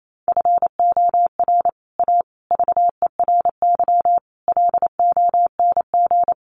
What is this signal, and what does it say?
Morse code, heard as a keyed tone, about 28 wpm, 700 Hz. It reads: FOR A VERY LONG
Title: Sets of 5 Words 28wpm